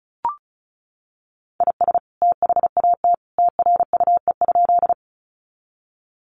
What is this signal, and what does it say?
Morse code, heard as a keyed tone, about 34 wpm, 700 Hz. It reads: IS THAT TRUE?
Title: Sentences from Top 400 Words 35wpm